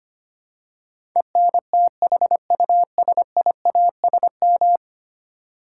Morse code, heard as a keyed tone, about 25 wpm, 700 Hz. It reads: ENTHUSIASM
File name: Ten Letter Words - 25wpm